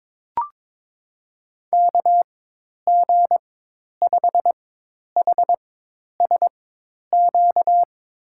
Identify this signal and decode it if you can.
Morse code, heard as a keyed tone, about 22 wpm, spaced as for 10 wpm, 700 Hz. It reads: KG5HSQ
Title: USA Call Signs - 4x Character Spacing - 22wpm